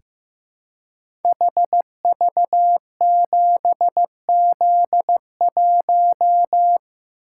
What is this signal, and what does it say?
Morse code, heard as a keyed tone, about 15 wpm, 700 Hz. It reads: HV7Z1
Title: Code Groups - 5 Letter:Numbers - 15wpm